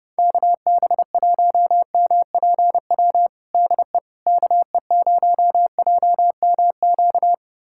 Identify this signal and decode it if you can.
Morse code, heard as a keyed tone, about 30 wpm, 700 Hz. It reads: KB1MPW DE KE0JMQ
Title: USA Callsign DE Callsign - 6 Characters - 30wpm